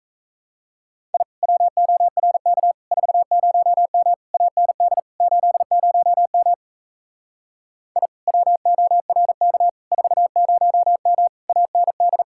Morse code, heard as a keyed tone, about 36 wpm, 700 Hz. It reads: I WORK 40M AND 80M I WORK 40M AND